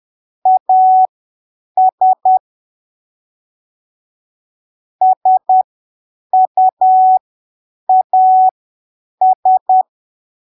Morse code, heard as a keyed tone, about 10 wpm, spaced as for 7 wpm, 750 Hz. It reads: AS SUAS
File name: arquivos / texto / texto 23 10ppm